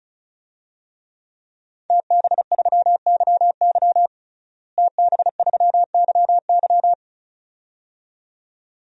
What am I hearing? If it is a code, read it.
Morse code, heard as a keyed tone, about 34 wpm, 700 Hz. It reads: TB3YY TB3YY
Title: DXCC-100 Call Signs - 5 Characters - Repeated 2x - 35wpm